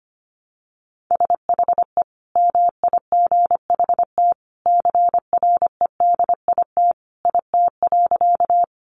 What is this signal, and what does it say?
Morse code, heard as a keyed tone, about 25 wpm, 700 Hz. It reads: SHE MIGHT CREDIT IT.
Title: Sentences from Top 1000 Words 25wpm